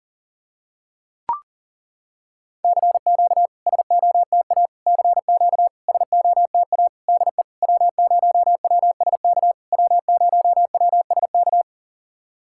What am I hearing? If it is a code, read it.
Morse code, heard as a keyed tone, about 40 wpm, 700 Hz. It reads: CQ SOTA CQ SOTA DE W0WSK W0WSK
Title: SOTA QSO Practice 40wpm